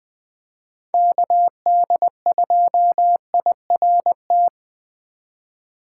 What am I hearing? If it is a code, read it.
Morse code, heard as a keyed tone, about 20 wpm, 700 Hz. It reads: KD2IRT
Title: USA Call Signs - 6 Characters - 2x Character Spacing - 20wpm